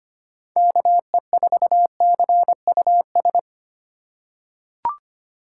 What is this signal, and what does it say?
Morse code, heard as a keyed tone, about 25 wpm, 700 Hz. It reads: KE4CUS
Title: USA Call Signs - 6 Characters - 2x Character Spacing - 25wpm